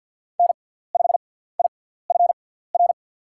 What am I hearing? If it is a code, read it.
Morse code, heard as a keyed tone, about 47 wpm, spaced as for 17 wpm, 700 Hz. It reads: N5IFR
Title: USA Call Signs - 6x Character Spacing - 50wpm